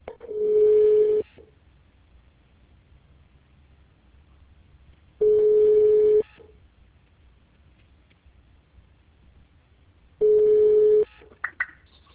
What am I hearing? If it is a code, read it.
Ringback tone (Europe, 425 Hz).